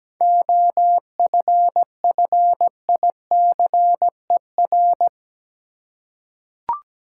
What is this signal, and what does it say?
Morse code, heard as a keyed tone, about 17 wpm, 700 Hz. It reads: OFFICER
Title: Seven Letter Words - 17wpm